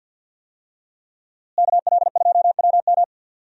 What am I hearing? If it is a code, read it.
Morse code, heard as a keyed tone, about 49 wpm, 700 Hz. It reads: KL2WM
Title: USA Call Signs - 5 Characters - Repeated 3x - 50wpm